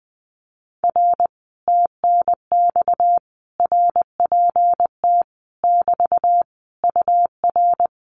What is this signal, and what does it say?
Morse code, heard as a keyed tone, about 20 wpm, 700 Hz. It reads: R TNX RPT <BT> UR
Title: Common QSO Phrases Encore 20wpm